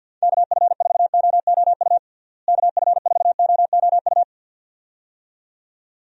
Morse code, heard as a keyed tone, about 52 wpm, 700 Hz. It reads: KF4OOU KF4OOU
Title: USA Call Signs - 6 Characters - Repeated 2x - 50wpm